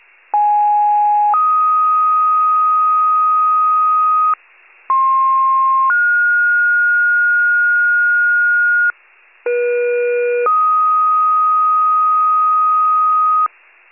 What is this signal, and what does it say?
Two-tone page, heard as a one-second tone, then a three-second tone, about 830 then 1250 Hz, about 1060 then 1430 Hz, about 480 then 1250 Hz. It is heard from 0.3 to 13.5 s.
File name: pager / 82 FIRE / wcalert